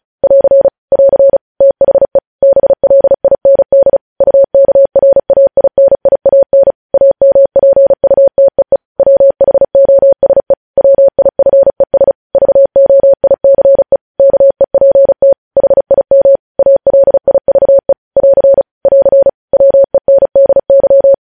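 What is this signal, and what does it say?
Morse code, heard as a keyed tone, about 36 wpm, 550 Hz. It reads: <AR> <AR> THE BLIND UKRAINIAN AMPUTEE WHOSE WIFES VOICE KEPT HIM ALIVE <AR> <AR> WENDY
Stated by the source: Morse code transmission of news headlines, preceded by vvv.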